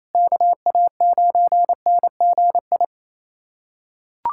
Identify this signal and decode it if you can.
Morse code, heard as a keyed tone, about 28 wpm, 700 Hz. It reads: KA9NGI
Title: USA Callsign DE Callsign 28wpm